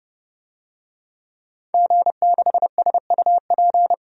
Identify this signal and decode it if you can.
Morse code, heard as a keyed tone, about 30 wpm, 700 Hz. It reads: G6SUP
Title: DXCC-100 Call Signs - 5 Characters - 30wpm